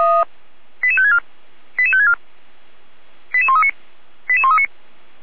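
Five-tone selective call (ZVEI), heard as quick runs of five tones, tones of about 70 ms. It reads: 88564 88564 88139 88139
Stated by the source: ZVEI five-tone call received on SDR.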